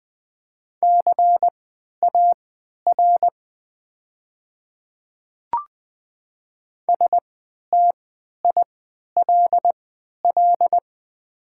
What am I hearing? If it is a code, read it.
Morse code, heard as a keyed tone, about 20 wpm, spaced as for 11 wpm, 690 Hz. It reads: CAR STILL
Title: 3x Character Spacing - Top 300 Words - 20wpm